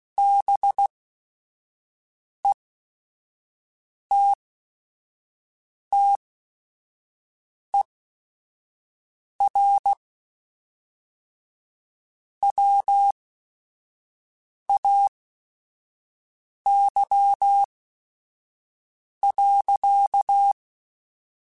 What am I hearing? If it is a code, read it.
Morse code, heard as a keyed tone, about 16 wpm, spaced as for 5 wpm, 800 Hz. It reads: BETTER WAY.